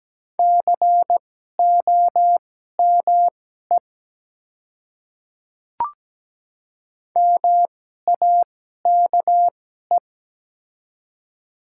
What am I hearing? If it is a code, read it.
Morse code, heard as a keyed tone, about 17 wpm, spaced as for 12 wpm, 700 Hz. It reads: COME MAKE
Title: 2x Character Spacing - Top 200 Words - 17wpm